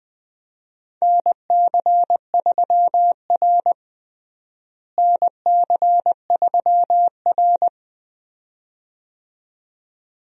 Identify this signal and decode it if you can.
Morse code, heard as a keyed tone, about 20 wpm, 700 Hz. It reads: NC3R NC3R
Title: USA Call Signs - 4 Characters - Repeated 2x - 20wpm